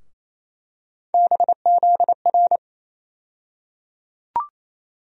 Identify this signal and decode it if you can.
Morse code, heard as a keyed tone, about 28 wpm, 700 Hz. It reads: BZR
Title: USA Call Sign Postfixes - 3 Letters 28wpm